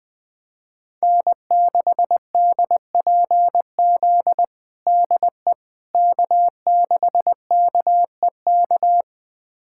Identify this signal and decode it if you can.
Morse code, heard as a keyed tone, about 20 wpm, 700 Hz. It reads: N6DPZ DE K6KEK
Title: USA Callsign DE Callsign - 5 Characters - 20wpm